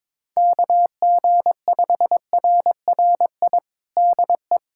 Morse code, heard as a keyed tone, about 22 wpm, 700 Hz. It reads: KG5RRI DE
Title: USA Callsign DE Callsign 22wpm